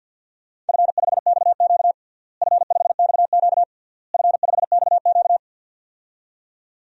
Morse code, heard as a keyed tone, about 48 wpm, 700 Hz. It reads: F5XQ F5XQ F5XQ
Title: DXCC-100 Call Signs - 4 Characters - Repeated 3x - 50wpm